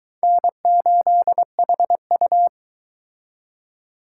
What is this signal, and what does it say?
Morse code, heard as a keyed tone, about 22 wpm, 700 Hz. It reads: N8HU